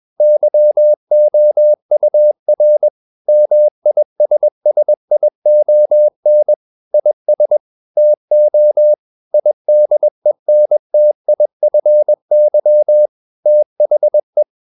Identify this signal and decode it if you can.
Morse code, heard as a keyed tone, about 21 wpm, 600 Hz. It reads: YOUR MISSION IS TO IDENTIFY THE
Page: CW med Gnister